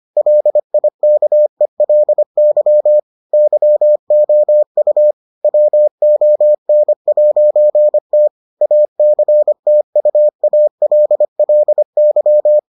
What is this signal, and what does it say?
Morse code, heard as a keyed tone, about 25 wpm, 600 Hz. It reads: LIKELY YOU WON'T ACTUALLY